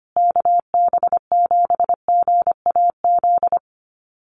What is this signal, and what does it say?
Morse code, heard as a keyed tone, about 25 wpm, 700 Hz. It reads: KB7GAZ
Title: USA Call Signs Repeated 2x 25wpm